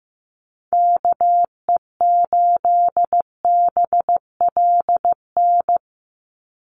Morse code, heard as a keyed tone, about 15 wpm, 700 Hz. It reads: KE8BLN
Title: USA Call Signs 15wpm